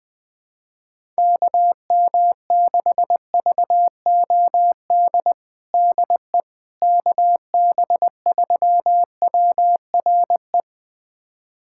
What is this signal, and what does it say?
Morse code, heard as a keyed tone, about 21 wpm, 700 Hz. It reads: KM6VOD DE KB3WRE
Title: USA Callsign DE Callsign 20wpm